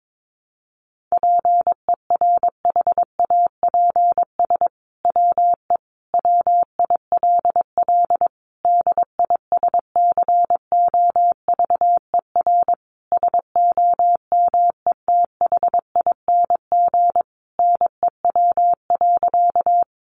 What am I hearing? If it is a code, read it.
Morse code, heard as a keyed tone, about 22 wpm, 700 Hz. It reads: PERHAPS WE WILL DISCOVER SOMETHING NEW.
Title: Sentences from Top 1100 Words 22wpm